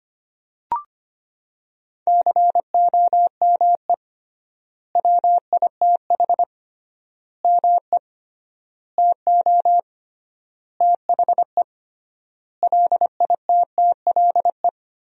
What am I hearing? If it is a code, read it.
Morse code, heard as a keyed tone, about 25 wpm, 700 Hz. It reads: COME WITH ME TO THE LITTLE
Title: 3x Word-Spacing - Sentences from Top 200 Words - 25wpm